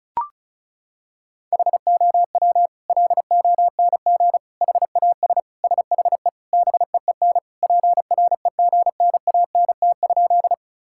Morse code, heard as a keyed tone, about 36 wpm, 700 Hz. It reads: HOW LONG HAS SHE BEEN PREGNANT?
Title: Sentences from Top 1000 Words - Review Part 6 - 35wpm